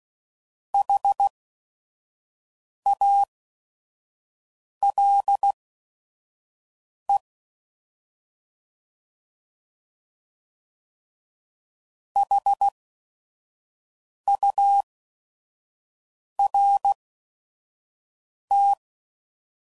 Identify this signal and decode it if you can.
Morse code, heard as a keyed tone, about 16 wpm, spaced as for 5 wpm, 800 Hz. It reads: HALE HURT
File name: Prefixes / By First Letter / Hs